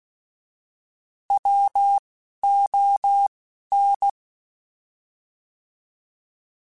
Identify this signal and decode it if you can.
Morse code, heard as a keyed tone, about 16 wpm, 800 Hz. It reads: W O N
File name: Alphabetical / Alphabetical mp3s / Ws @10